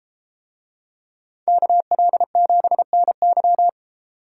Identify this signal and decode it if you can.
Morse code, heard as a keyed tone, about 32 wpm, 700 Hz. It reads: KL7NY